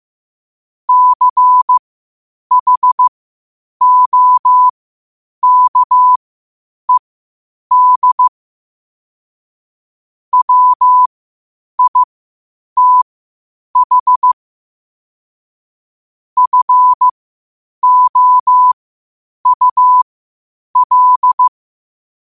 Morse code, heard as a keyed tone, about 15 wpm, spaced as for 8 wpm, 1000 Hz. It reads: CHOKED WITH FOUL
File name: qotdfiles / 8wpm